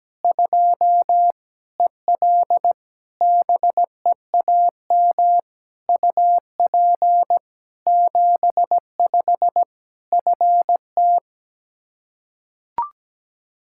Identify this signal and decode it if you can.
Morse code, heard as a keyed tone, about 17 wpm, 700 Hz. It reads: 2 EL BEAM UP 75 FT
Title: Practice QSO - Exchange 2 - Info - 17wpm